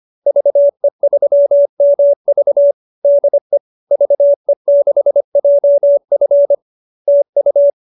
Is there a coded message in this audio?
Yes: Morse code from 0.3 to 7.7 s, heard as a keyed tone, about 25 wpm, 590 Hz. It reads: VE3MV DE VE6JF TU